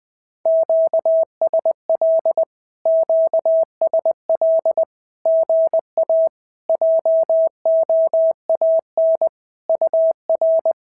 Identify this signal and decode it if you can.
Morse code, heard as a keyed tone, about 20 wpm, 650 Hz. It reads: QSL QSL GA JOAN UR
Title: SOTA QSO Practice 20wpm